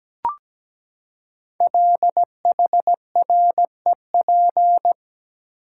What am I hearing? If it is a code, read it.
Morse code, heard as a keyed tone, about 17 wpm, 700 Hz. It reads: LHREP
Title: Code Groups - 5 Letters - 17wpm